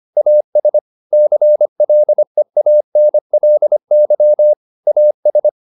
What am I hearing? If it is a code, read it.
Morse code, heard as a keyed tone, about 25 wpm, 600 Hz. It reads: AS CLEANLY AS